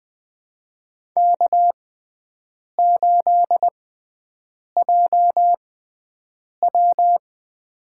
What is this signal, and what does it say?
Morse code, heard as a keyed tone, about 20 wpm, spaced as for 7 wpm, 700 Hz. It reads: K8JW
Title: USA Call Signs - 4 Characters - 6x Character Spacing - 20wpm